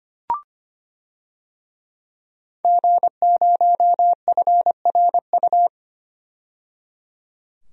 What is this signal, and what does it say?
Morse code, heard as a keyed tone, about 25 wpm, 700 Hz. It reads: G0FRU
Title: DXCC-100 Call Signs - 5 Characters - 25wpm